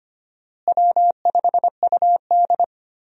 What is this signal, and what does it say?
Morse code, heard as a keyed tone, about 25 wpm, 700 Hz. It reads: W5UD
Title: USA Call Signs - 4 Characters - Repeated 3x - 25wpm